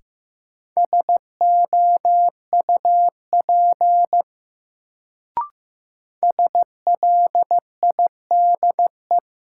Morse code, heard as a keyed tone, about 15 wpm, 700 Hz. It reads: SOUP SLIDE
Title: Sentences from Top 1600 Words - New Word Drill Alt - 15wpm